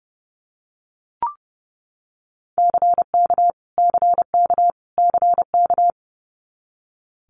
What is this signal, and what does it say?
Morse code, heard as a keyed tone, about 30 wpm, 700 Hz. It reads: CK CK CK
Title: Multi-letter Phonemes Repeated 3x 30wpm